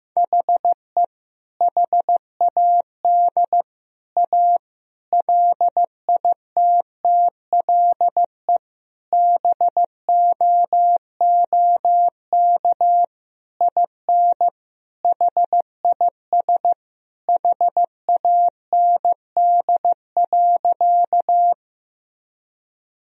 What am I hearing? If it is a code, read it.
Morse code, heard as a keyed tone, about 15 wpm, 700 Hz. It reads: HE HAD A LITTLE BOOK IN HIS HAND.
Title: Sentences from Top 1000 Words - Review Part 1 - 15wpm